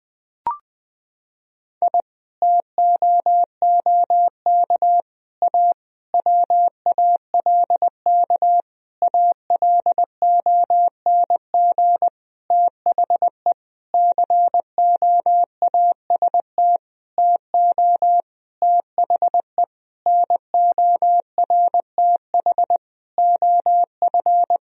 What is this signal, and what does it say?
Morse code, heard as a keyed tone, about 20 wpm, 700 Hz. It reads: I TOOK A WALK ALONG THE COAST TO THE NORTH OF
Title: Sentences from Top 1000 Words - Review Part 4 - 20wpm